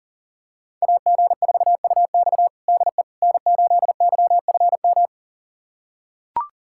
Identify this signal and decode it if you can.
Morse code, heard as a keyed tone, about 40 wpm, 700 Hz. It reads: AG4UX DE N8YFM
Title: USA Callsign DE Callsign - 5 Characters - 40wpm